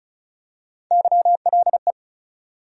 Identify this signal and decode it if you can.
Morse code, heard as a keyed tone, about 36 wpm, 700 Hz. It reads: YLE
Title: USA Call Sign Postfixes - 3 Letters 35wpm